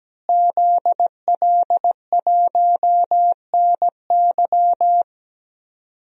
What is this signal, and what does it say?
Morse code, heard as a keyed tone, about 17 wpm, 700 Hz. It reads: ZL1NY
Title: DXCC-100 Call Signs - 5 Characters - Repeated 3x - 17wpm